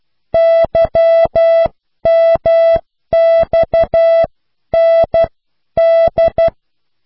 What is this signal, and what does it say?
Morse code, heard as a keyed tone, about 12 wpm, spaced as for 10 wpm, 660 Hz. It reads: YMXND